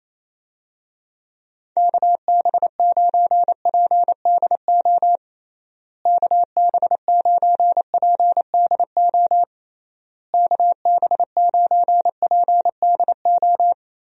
Morse code, heard as a keyed tone, about 27 wpm, 700 Hz. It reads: KB9PDO KB9PDO KB9PDO
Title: USA Call Signs - 6 Characters - Repeated 3x - 28wpm